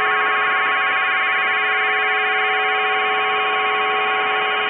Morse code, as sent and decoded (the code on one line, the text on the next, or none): none